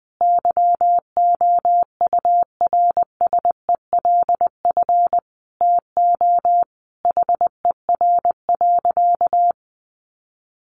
Morse code, heard as a keyed tone, about 20 wpm, 700 Hz. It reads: YOURSELF TO HER.
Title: Sentences from Top 1100 Words 20wpm